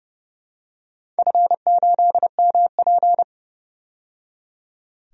Morse code, heard as a keyed tone, about 30 wpm, 700 Hz. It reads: F8MP
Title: DXCC-100 Call Signs - 4 Characters - 30wpm